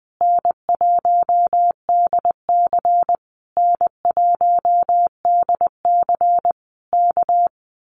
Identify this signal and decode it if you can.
Morse code, heard as a keyed tone, about 20 wpm, 700 Hz. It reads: N1DC N1DC K
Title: Common QSO Phrases Encore 20wpm